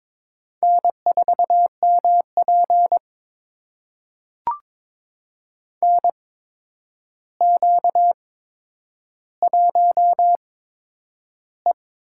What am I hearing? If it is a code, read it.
Morse code, heard as a keyed tone, about 22 wpm, spaced as for 9 wpm, 700 Hz. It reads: N4MP NQ1E
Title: USA Call Signs - 4 Characters - 8x Character Spacing - 22wpm